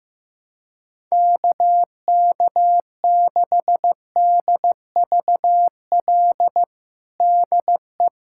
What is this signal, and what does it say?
Morse code, heard as a keyed tone, about 15 wpm, 700 Hz. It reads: KK6DVL DE
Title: USA Callsign DE Callsign - 6 Characters - 15wpm